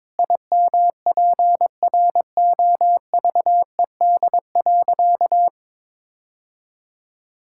Morse code, heard as a keyed tone, about 21 wpm, 700 Hz. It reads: IMPROVED.
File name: practice / 22wpm / Sentences with 4-Word Phrases - 22wpm